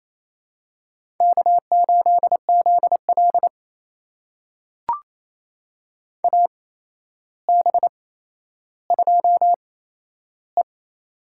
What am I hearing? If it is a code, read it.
Morse code, heard as a keyed tone, about 28 wpm, spaced as for 12 wpm, 700 Hz. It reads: K8ZL AB2E
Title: USA Call Signs - 4 Characters - 8x Character Spacing - 28wpm